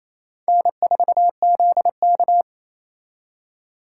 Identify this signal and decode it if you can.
Morse code, heard as a keyed tone, about 34 wpm, 700 Hz. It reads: N4ZK
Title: Speed Racing - USA 1x2 Call Signs 28wpm